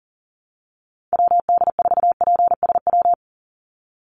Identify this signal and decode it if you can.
Morse code, heard as a keyed tone, about 40 wpm, 700 Hz. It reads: WD4PSW
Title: USA Call Signs 40wpm